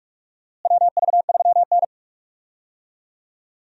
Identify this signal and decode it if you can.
Morse code, heard as a keyed tone, about 46 wpm, 700 Hz. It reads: WV3N
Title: USA Call Signs - 4 Characters - Repeated 3x - 45wpm